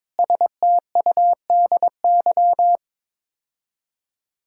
Morse code, heard as a keyed tone, about 22 wpm, 700 Hz. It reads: STUDY
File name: practice / 22wpm / Build A Phrase - 5-Word Phrases - 22wpm